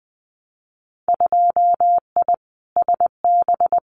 Morse code, heard as a keyed tone, about 20 wpm, 700 Hz. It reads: 2I SB
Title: Winter Field Day Exchanges Encore 20wpm